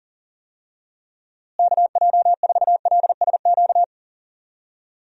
Morse code, heard as a keyed tone, about 40 wpm, 700 Hz. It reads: KJ4LSQ
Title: USA Call Signs - 6 Characters - Repeated 2x - 40wpm